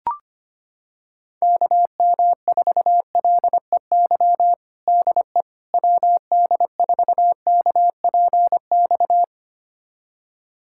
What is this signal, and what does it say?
Morse code, heard as a keyed tone, about 25 wpm, 700 Hz. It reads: KM4LEY DE WD4KPX
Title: USA Callsign DE Callsign 25wpm